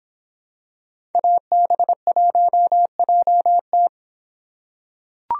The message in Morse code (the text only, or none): AB1JT